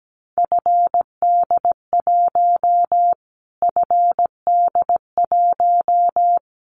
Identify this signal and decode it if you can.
Morse code, heard as a keyed tone, about 17 wpm, 700 Hz. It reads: FD1 FD1
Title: Callsign Prefixes for Top 25 DXCC - Repeated 3x - 17wpm